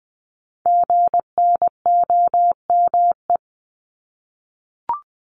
Morse code, heard as a keyed tone, about 20 wpm, 700 Hz. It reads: GNOME
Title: Multi-letter Phonemes Repeated 3x 20wpm